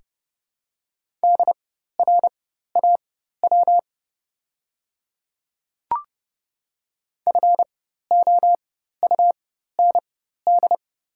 Morse code, heard as a keyed tone, about 30 wpm, spaced as for 14 wpm, 700 Hz. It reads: DRAW FOUND
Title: 4x Character Spacing - Top 300 Words - 30wpm